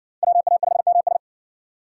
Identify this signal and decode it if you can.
Morse code, heard as a keyed tone, about 64 wpm, 700 Hz. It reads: WA5MS